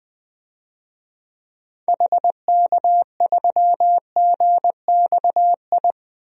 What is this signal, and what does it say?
Morse code, heard as a keyed tone, about 21 wpm, 700 Hz. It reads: HK3GXI
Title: DXCC-100 Call Signs - 6 Characters - 20wpm